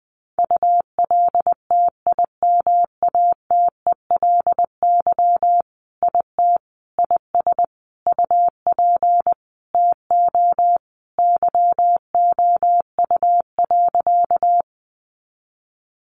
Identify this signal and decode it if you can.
Morse code, heard as a keyed tone, about 20 wpm, 700 Hz. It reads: ULTIMATELY IT IS UP TO YOU.
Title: Sentences from Top 1200 Words 20wpm